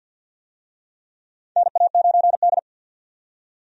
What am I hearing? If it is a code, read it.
Morse code, heard as a keyed tone, about 50 wpm, 700 Hz. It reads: NA9D
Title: USA Call Signs - 4 Characters - 50wpm